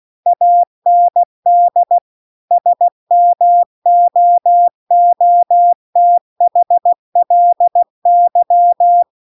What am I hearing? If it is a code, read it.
Morse code, heard as a keyed tone, about 16 wpm, 700 Hz. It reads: AND SMOOTHLY
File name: CW-The-Big-Twoheaded-River-2-16WPM